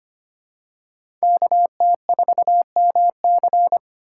Morse code, heard as a keyed tone, about 25 wpm, 700 Hz. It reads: KT4MC
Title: USA Call Signs - 5 Characters - 4x Character Spacing - 25wpm